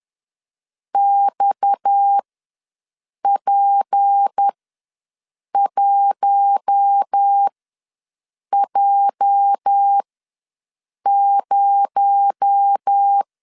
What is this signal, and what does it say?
Morse code, heard as a keyed tone, about 10 wpm, spaced as for 6 wpm, 800 Hz. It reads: XP1J0
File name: LEZIONI / 10 SIGNS / 10B INT ws 10wpm 4134